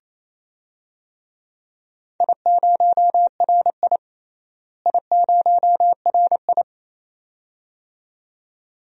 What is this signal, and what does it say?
Morse code, heard as a keyed tone, about 28 wpm, 700 Hz. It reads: I0RI I0RI
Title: DXCC-100 Call Signs - 4 Characters - Repeated 2x - 28wpm